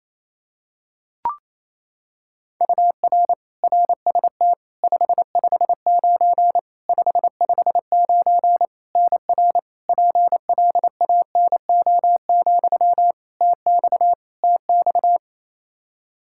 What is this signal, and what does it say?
Morse code, heard as a keyed tone, about 28 wpm, 700 Hz. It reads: UR RST 559 559 NR PLANO, TX TX
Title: Practice QSO - Exchange 1 - Beginning - 28wpm